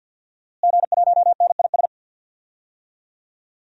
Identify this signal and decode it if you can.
Morse code, heard as a keyed tone, about 50 wpm, 700 Hz. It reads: G1NIS
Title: DXCC-100 Call Signs - 5 Characters - Repeated 3x - 50wpm